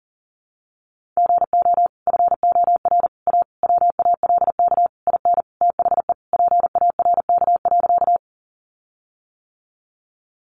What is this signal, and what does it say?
Morse code, heard as a keyed tone, about 37 wpm, 700 Hz. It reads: GO FOR A WALK IN THE PARK.
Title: Sentences from Top 1300 Words 40wpm